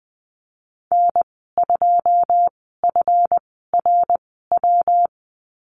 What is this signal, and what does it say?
Morse code, heard as a keyed tone, about 20 wpm, spaced as for 14 wpm, 700 Hz. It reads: N2FRW
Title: 147 Lesson 24g - Letter y - Call Signs 20wpm